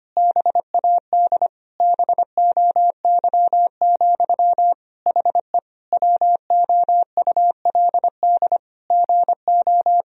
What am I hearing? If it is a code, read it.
Morse code, heard as a keyed tone, about 25 wpm, 700 Hz. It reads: BAD BOY, HE WOULD GO
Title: Sentences from Top 1000 Words - Review Part 3 - 25wpm